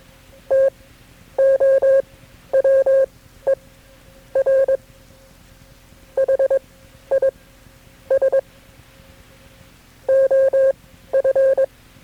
Morse code, heard as a keyed tone, about 21 wpm, spaced as for 11 wpm, 540 Hz. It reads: TOWER HIS OF
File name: KB6MT-tape3-side1